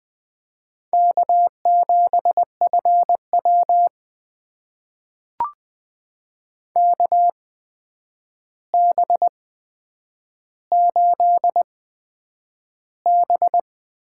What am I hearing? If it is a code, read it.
Morse code, heard as a keyed tone, about 20 wpm, spaced as for 9 wpm, 700 Hz. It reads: K7FW KB8B
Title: USA Call Signs - 4 Characters - 8x Character Spacing - 20wpm